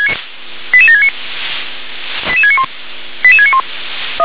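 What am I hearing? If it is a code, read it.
Five-tone selective call (ZVEI), heard as quick runs of five tones, tones of about 70 ms. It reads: 88769 88691 88691
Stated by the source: ZVEI five-tone selective call 88702 received on SDR.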